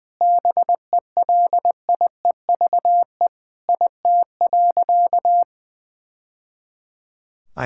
Morse code, heard as a keyed tone, about 21 wpm, 700 Hz. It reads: BELIEVE IT.
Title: Sentences from Top 1000 Words - Review Part 6 - 20wpm